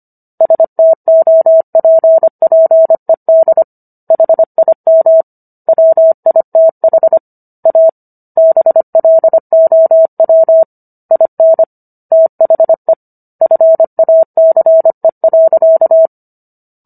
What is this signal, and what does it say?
Morse code, heard as a keyed tone, about 25 wpm, 650 Hz. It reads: STOPPED HIM WITH A BLOW IN THE FACE.